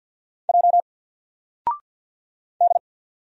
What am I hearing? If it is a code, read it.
Morse code, heard as a keyed tone, about 50 wpm, 700 Hz. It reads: J D
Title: Single Letter-Number 50wpm